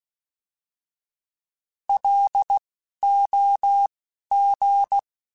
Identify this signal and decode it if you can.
Morse code, heard as a keyed tone, about 16 wpm, spaced as for 12 wpm, 800 Hz. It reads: LOG
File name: fromSMs / FromSMs mp3s / Antennas@10